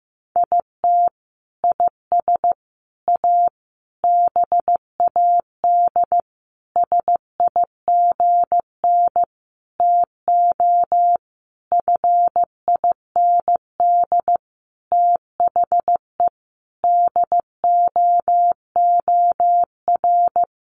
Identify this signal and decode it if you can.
Morse code, heard as a keyed tone, about 15 wpm, 700 Hz. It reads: IT IS A BAD SIGN TO FIND THE DOOR
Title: Sentences from Top 1000 Words 15wpm